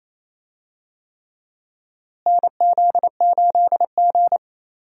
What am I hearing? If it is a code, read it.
Morse code, heard as a keyed tone, about 28 wpm, 700 Hz. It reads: NZ8G
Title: USA Call Signs - 4 Characters - Repeated 3x - 28wpm